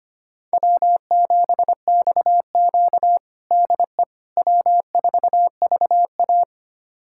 Morse code, heard as a keyed tone, about 25 wpm, 700 Hz. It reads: W7XQ DE W4VA
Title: USA Callsign DE Callsign - 4 Characters - 25wpm